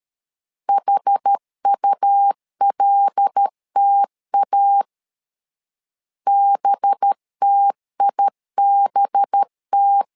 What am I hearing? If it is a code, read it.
Morse code, heard as a keyed tone, about 13 wpm, 800 Hz. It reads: HULTA BTIBT
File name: LEZIONI / 5 LBVC / 5 LB 12wpm 1906